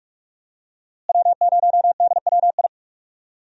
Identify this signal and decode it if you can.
Morse code, heard as a keyed tone, about 43 wpm, 700 Hz. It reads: W0DWI
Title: USA Call Signs - 5 Characters - Repeated 2x - 45wpm